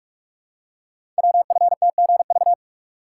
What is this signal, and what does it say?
Morse code, heard as a keyed tone, about 44 wpm, 700 Hz. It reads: WFTGV
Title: Code Groups - 5 Letters - 45wpm